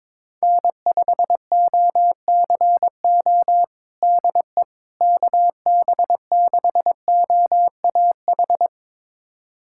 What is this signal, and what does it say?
Morse code, heard as a keyed tone, about 21 wpm, 700 Hz. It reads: N5OCO DE KB6OAH
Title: USA Callsign DE Callsign 22wpm